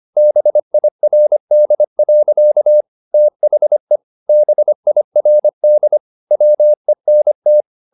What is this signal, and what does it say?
Morse code, heard as a keyed tone, about 25 wpm, 600 Hz. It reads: BIRD. THE BIRD WENT
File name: the old man and the sea part 1 25wpm